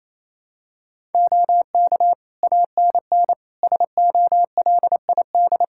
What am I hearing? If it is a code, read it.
Morse code, heard as a keyed tone, about 28 wpm, 700 Hz. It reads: OK ANN SOLID
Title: K1USN SST QSO Practice 28wpm